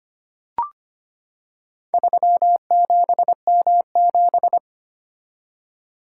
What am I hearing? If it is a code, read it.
Morse code, heard as a keyed tone, about 25 wpm, 700 Hz. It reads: 37M7
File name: practice / 25wpm / Four Letter-Number Combinations - 25wpm